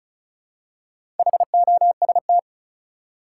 Morse code, heard as a keyed tone, about 36 wpm, 700 Hz. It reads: HOST
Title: Four Letter Words - 35wpm